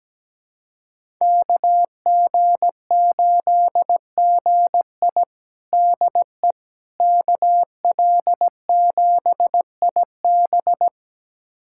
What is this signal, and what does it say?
Morse code, heard as a keyed tone, about 17 wpm, 700 Hz. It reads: KG8GI DE KL7IB
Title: USA Callsign DE Callsign 17wpm